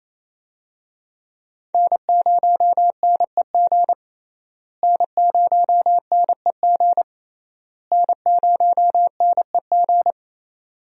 Morse code, heard as a keyed tone, about 28 wpm, 700 Hz. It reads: N0NEG N0NEG N0NEG
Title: USA Call Signs - 5 Characters - Repeated 3x - 28wpm